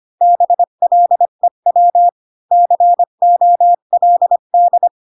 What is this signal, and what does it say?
Morse code, heard as a keyed tone, about 25 wpm, 700 Hz. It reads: BLEW COLD
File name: Pinocchio / Pinocchio ch6 20 25 ik1hge